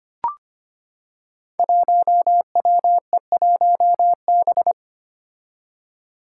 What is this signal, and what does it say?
Morse code, heard as a keyed tone, about 25 wpm, 700 Hz. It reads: 1WE1B
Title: Five Challenging Letter-Number Combinations 25wpm